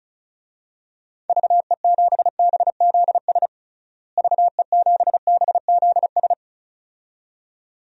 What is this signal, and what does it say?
Morse code, heard as a keyed tone, about 36 wpm, 700 Hz. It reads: VE7BZS VE7BZS
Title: DXCC-100 Call Signs - 6 Characters - Repeated 2x - 35wpm